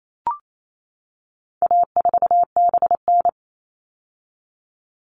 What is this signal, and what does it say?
Morse code, heard as a keyed tone, about 28 wpm, 700 Hz. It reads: A4BN
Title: All Cut Numbers - 4-Digit 28wpm